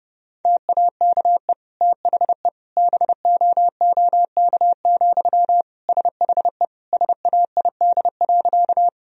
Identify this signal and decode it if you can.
Morse code, heard as a keyed tone, about 30 wpm, 700 Hz. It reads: TAKE THE BOOK, SHE SAID.
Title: Sentences from Top 1000 Words - Review Part 1 - 30wpm